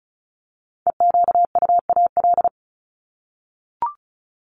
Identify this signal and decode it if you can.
Morse code, heard as a keyed tone, about 34 wpm, 700 Hz. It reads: EQUAL
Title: LY Suffix - Top 100 Words with Base Words 35wpm